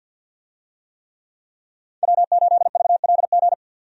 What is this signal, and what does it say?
Morse code, heard as a keyed tone, about 51 wpm, 700 Hz. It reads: W8VLG
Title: USA Call Signs - 5 Characters - 50wpm